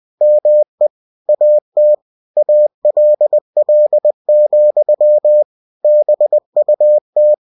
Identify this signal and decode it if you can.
Morse code, heard as a keyed tone, about 20 wpm, 600 Hz. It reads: ME AT ALL, BUT
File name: cw-books / winnie-the-pooh / winnie-20wpm-page-0048